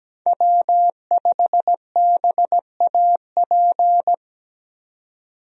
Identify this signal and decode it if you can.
Morse code, heard as a keyed tone, about 17 wpm, 700 Hz. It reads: W5BAP
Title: USA Call Signs - 5 Characters - 17wpm